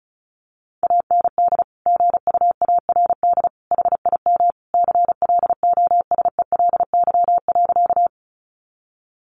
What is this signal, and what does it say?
Morse code, heard as a keyed tone, about 34 wpm, 700 Hz. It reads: AND GUARD HIM CLOSELY.
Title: Sentences from Top 1400 Words 35wpm